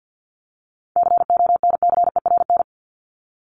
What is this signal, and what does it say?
Morse code, heard as a keyed tone, about 48 wpm, 700 Hz. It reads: CONCERN
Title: ED Suffix - Top 100 Words with Base Words 50wpm